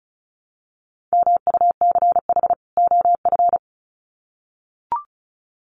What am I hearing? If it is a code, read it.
Morse code, heard as a keyed tone, about 34 wpm, 700 Hz. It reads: MUCH OF
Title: Sets of 2 Words 35wpm